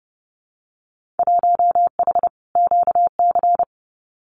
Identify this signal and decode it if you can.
Morse code, heard as a keyed tone, about 30 wpm, 700 Hz. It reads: 1H QC
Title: Winter Field Day Exchanges 30wpm